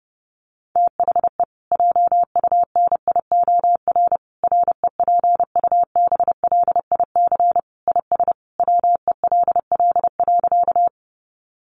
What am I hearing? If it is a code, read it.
Morse code, heard as a keyed tone, about 30 wpm, 700 Hz. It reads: THE JUNIOR REPUBLIC IS WELL.
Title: Sentences from Top 1500 Words 30wpm